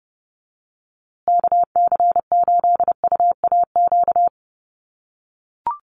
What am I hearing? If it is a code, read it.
Morse code, heard as a keyed tone, about 30 wpm, 700 Hz. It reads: KC8UAQ
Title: USA Call Signs Repeated 3x 30wpm